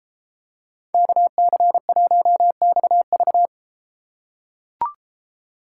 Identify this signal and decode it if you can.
Morse code, heard as a keyed tone, about 31 wpm, 700 Hz. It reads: KC1XV